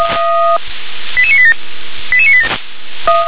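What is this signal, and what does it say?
Five-tone selective call (ZVEI), heard as quick runs of five tones, tones of about 70 ms. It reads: 88987 88987
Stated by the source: ZVEI five-tone selective call received on an SDR.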